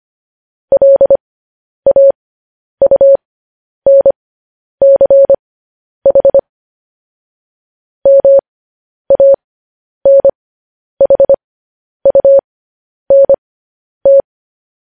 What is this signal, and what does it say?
Morse code, heard as a keyed tone, about 25 wpm, spaced as for 10 wpm, 550 Hz. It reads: LAUNCH MANHUNT
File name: podcast / News headlines-1x-25x10